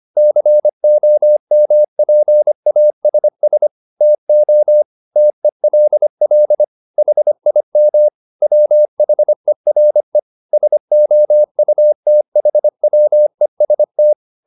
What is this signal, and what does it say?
Morse code, heard as a keyed tone, about 25 wpm, 600 Hz. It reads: COMPASS TO TELL HIM WHERE SOUTHWEST